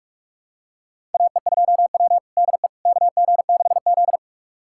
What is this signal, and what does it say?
Morse code, heard as a keyed tone, about 46 wpm, 700 Hz. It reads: AE2W DE KG6Z
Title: USA Callsign DE Callsign - 4 Characters - 45wpm